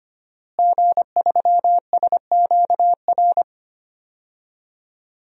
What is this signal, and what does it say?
Morse code, heard as a keyed tone, about 25 wpm, 700 Hz. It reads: G3SQR
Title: DXCC-100 Call Signs - 5 Characters - Repeated 2x - 25wpm